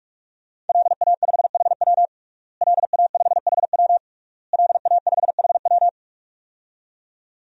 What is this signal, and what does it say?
Morse code, heard as a keyed tone, about 46 wpm, 700 Hz. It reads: LA5HW LA5HW LA5HW
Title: DXCC-100 Call Signs - 5 Characters - Repeated 3x - 45wpm